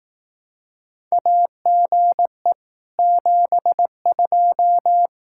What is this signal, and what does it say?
Morse code, heard as a keyed tone, about 18 wpm, 700 Hz. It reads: AGE 72
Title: Speed-Racing Common QSO Phrases Encore - 15wpm